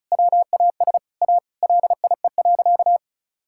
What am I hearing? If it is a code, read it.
Morse code, heard as a keyed tone, about 36 wpm, 700 Hz. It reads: WAS A LIE.
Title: Sentences from Top 1600 Words 35wpm